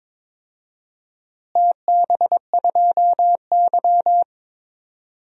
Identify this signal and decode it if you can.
Morse code, heard as a keyed tone, about 22 wpm, 700 Hz. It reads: TB2Y
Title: DXCC-100 Call Signs - 4 Characters - Repeated 2x - 22wpm